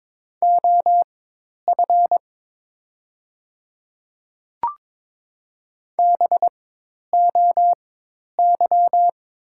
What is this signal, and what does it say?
Morse code, heard as a keyed tone, about 22 wpm, spaced as for 10 wpm, 690 Hz. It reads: OF BOY
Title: 4x Character Spacing - Top 200 Words - 22wpm